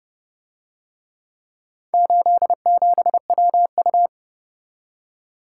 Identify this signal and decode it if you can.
Morse code, heard as a keyed tone, about 28 wpm, 700 Hz. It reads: 87WU